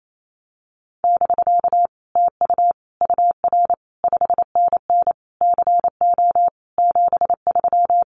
Morse code, heard as a keyed tone, about 28 wpm, 700 Hz. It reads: <BK> TU UR 5NN CO 73
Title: Common QSO Phrases Encore 28wpm